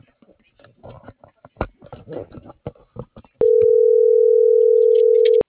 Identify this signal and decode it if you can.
Ringback tone (North America), 440 + 480 Hz.